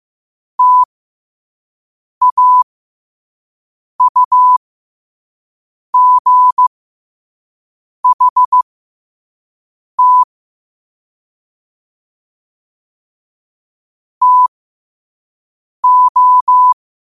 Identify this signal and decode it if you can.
Morse code, heard as a keyed tone, about 15 wpm, spaced as for 5 wpm, 1000 Hz. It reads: TAUGHT TO
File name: qotdfiles / 5wpm